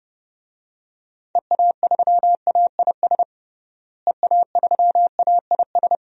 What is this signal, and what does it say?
Morse code, heard as a keyed tone, about 32 wpm, 700 Hz. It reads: EA3AIS EA3AIS
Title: DXCC-100 Call Signs - 6 Characters - Repeated 2x - 30wpm